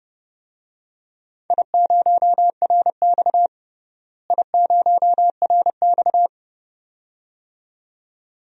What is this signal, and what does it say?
Morse code, heard as a keyed tone, about 32 wpm, 700 Hz. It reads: I0RX I0RX
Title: DXCC-100 Call Signs - 4 Characters - Repeated 2x - 30wpm